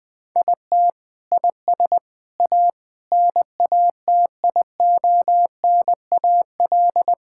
Morse code, heard as a keyed tone, about 20 wpm, 700 Hz. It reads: IT IS A NATIONAL
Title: Sentences from Top 1000 Words - Review Part 5 - 20wpm